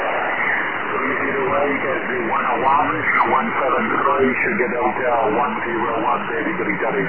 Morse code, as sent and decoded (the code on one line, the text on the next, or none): none